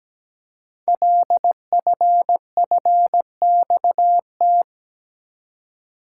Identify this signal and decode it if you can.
Morse code, heard as a keyed tone, about 17 wpm, 700 Hz. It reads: LFFXT
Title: Code Groups - 5 Letters - 17wpm